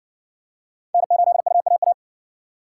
Morse code, heard as a keyed tone, about 60 wpm, 700 Hz. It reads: N8UAA